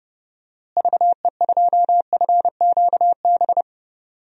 Morse code, heard as a keyed tone, about 30 wpm, 700 Hz. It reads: VE2FQB
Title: DXCC-100 Call Signs - 6 Characters - Repeated 2x - 30wpm